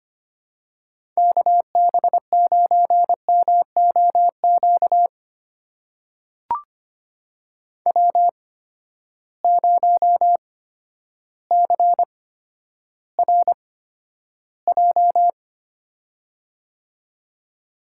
Morse code, heard as a keyed tone, about 26 wpm, spaced as for 12 wpm, 700 Hz. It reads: KB9MOQ W0CRJ
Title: USA Call Signs - 8x Character Spacing - 25wpm